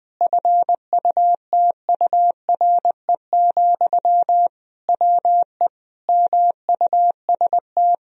Morse code, heard as a keyed tone, about 21 wpm, 700 Hz. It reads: FUTURE, WE MUST
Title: Sentences from Top 1000 Words - Review Part 3 - 20wpm